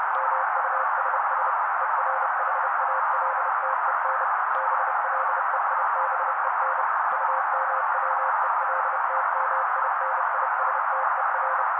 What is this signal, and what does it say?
Morse code, heard as a keyed tone, about 30 wpm, 530 Hz. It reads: MUSS ERHALTEN BLEIBEN UMWELTMINISTER
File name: daily / 29wpm noise